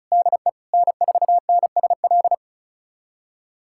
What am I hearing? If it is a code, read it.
Morse code, heard as a keyed tone, about 36 wpm, 700 Hz. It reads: DE N4NSL
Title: USA Callsign DE Callsign - 5 Characters - 35wpm